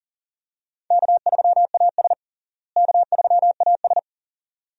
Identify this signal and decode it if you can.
Morse code, heard as a keyed tone, about 40 wpm, 700 Hz. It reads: K3AS K3AS
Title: USA Call Signs - 4 Characters - Repeated 2x - 40wpm